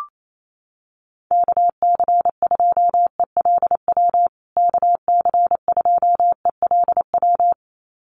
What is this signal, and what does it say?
Morse code, heard as a keyed tone, about 28 wpm, 700 Hz. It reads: KC2ELW KC2ELW
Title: USA Call Signs Repeated 2x 28wpm